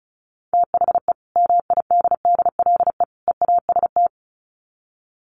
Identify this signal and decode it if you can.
Morse code, heard as a keyed tone, about 34 wpm, 700 Hz. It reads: THE MIDDLE EAST
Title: Sets of 3 Words 35wpm